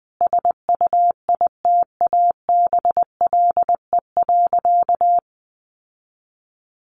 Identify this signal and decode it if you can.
Morse code, heard as a keyed tone, about 19 wpm, 700 Hz. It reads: SUITABLE.
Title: Sentences from Top 1400 Words 20wpm